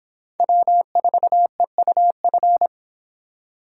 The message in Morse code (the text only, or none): W4EUF